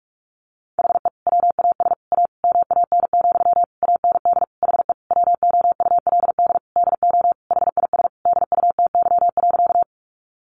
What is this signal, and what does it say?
Morse code, heard as a keyed tone, about 44 wpm, 700 Hz. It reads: HE WAS A MAN, AND HE WOULD DO HIS DUTY.
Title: Sentences from Top 1500 Words 45wpm